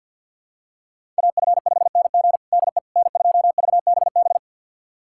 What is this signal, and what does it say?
Morse code, heard as a keyed tone, about 46 wpm, 700 Hz. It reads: AF5NG DE N2VBB
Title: USA Callsign DE Callsign - 5 Characters - 50wpm